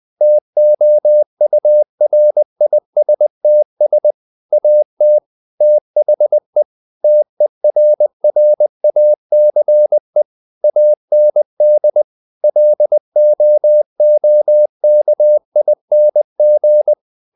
Morse code, heard as a keyed tone, about 20 wpm, 600 Hz. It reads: TOURISTS AT THE TERRACE AND LOOKING
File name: the old man and the sea part 2 20wpm